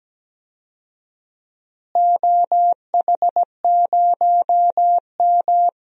Morse code, heard as a keyed tone, about 17 wpm, 700 Hz. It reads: OH0M
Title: DXCC-100 Call Signs - 4 Characters - Repeated 3x - 17wpm